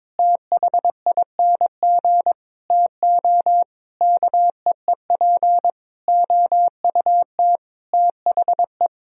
Morse code, heard as a keyed tone, about 22 wpm, 700 Hz. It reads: THING TO KEEP OUT THE
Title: Sentences from Top 1000 Words - Review Part 2 - 22wpm